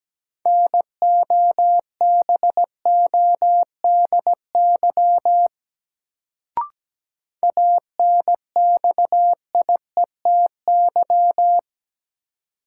Morse code, heard as a keyed tone, about 17 wpm, 700 Hz. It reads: NOBODY ANXIETY
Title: Sentences from Top 1600 Words - New Word Drill Alt - 17wpm